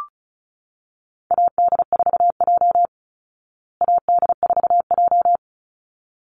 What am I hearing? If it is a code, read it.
Morse code, heard as a keyed tone, about 34 wpm, 700 Hz. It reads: AD4J AD4J
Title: Call Signs Repeated 3x - 35wpm